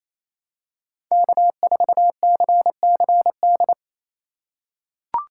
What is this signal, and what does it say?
Morse code, heard as a keyed tone, about 28 wpm, 700 Hz. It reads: K4CCD
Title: USA Call Signs - 5 Characters - 4x Character Spacing - 28wpm